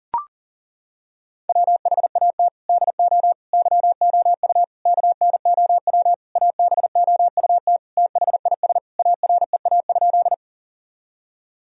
Morse code, heard as a keyed tone, about 40 wpm, 700 Hz. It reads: WHAT DO YOU KNOW ABOUT THIS AREA?
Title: Sentences from Top 1000 Words - Review Part 2 - 40wpm